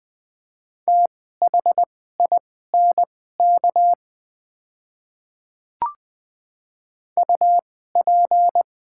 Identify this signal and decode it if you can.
Morse code, heard as a keyed tone, about 20 wpm, spaced as for 14 wpm, 700 Hz. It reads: THINK UP
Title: 2x Character Spacing - Top 200 Words - 20wpm